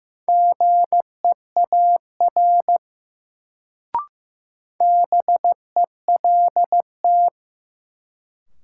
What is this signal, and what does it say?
Morse code, heard as a keyed tone, about 15 wpm, 700 Hz. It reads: GEAR BELT
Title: Sentences from Top 1400 Words - New Word Drill - 15wpm